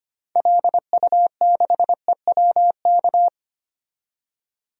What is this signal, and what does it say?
Morse code, heard as a keyed tone, about 25 wpm, 700 Hz. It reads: LU6EWK
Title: DXCC-100 Call Signs - 6 Characters - 25wpm